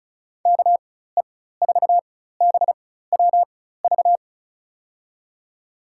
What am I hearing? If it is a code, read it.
Morse code, heard as a keyed tone, about 34 wpm, spaced as for 16 wpm, 700 Hz. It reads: KE4BWV
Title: USA Call Signs - 6 Characters - 4x Character Spacing - 35wpm